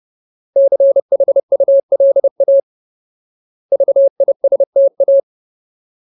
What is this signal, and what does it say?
Morse code, heard as a keyed tone, about 30 wpm, 550 Hz. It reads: CHULA VISTA
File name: Ultimate Word List / 30WPM Ultimate